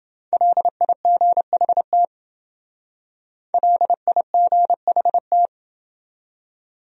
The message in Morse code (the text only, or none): LIGHT LIGHT